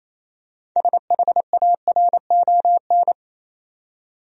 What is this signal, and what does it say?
Morse code, heard as a keyed tone, about 28 wpm, 700 Hz. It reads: SHARON
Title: USA Common Names - 28wpm